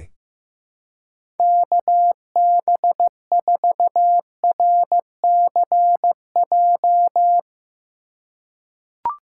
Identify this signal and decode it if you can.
Morse code, heard as a keyed tone, about 15 wpm, 700 Hz. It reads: KB4RCJ
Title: USA Call Signs - 6 Characters - 3x Character Spacing - 15wpm